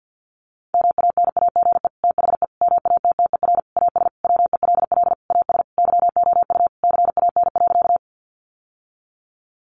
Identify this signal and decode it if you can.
Morse code, heard as a keyed tone, about 50 wpm, 700 Hz. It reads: MANAGE THE MATTER AS WELL AS YOU CAN.
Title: Sentences from Top 1400 Words 50wpm